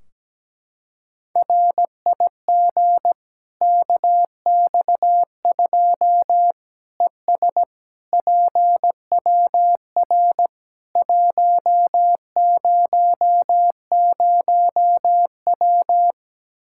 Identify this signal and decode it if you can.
Morse code, heard as a keyed tone, about 17 wpm, 700 Hz. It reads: RIG KX2 ES PWR 100W
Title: K1USN SST QSO Practice 17wpm